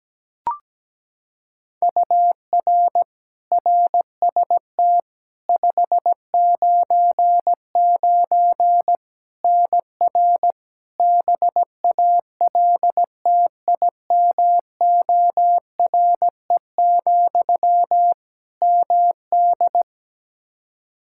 Morse code, heard as a keyed tone, about 17 wpm, 700 Hz. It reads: UR RST 599 NR BALTIMORE, MD
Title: Practice QSO - Exchange 1 - Beginning - 17wpm